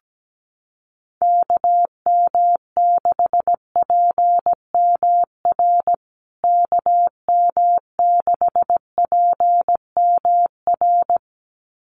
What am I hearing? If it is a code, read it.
Morse code, heard as a keyed tone, about 17 wpm, 700 Hz. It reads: KM6PMR KM6PMR
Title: USA Call Signs Repeated 2x 17wpm